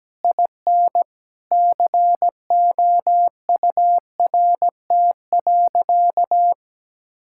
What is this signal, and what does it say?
Morse code, heard as a keyed tone, about 17 wpm, 700 Hz. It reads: IN COURT.
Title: Sentences from Top 1600 Words 17wpm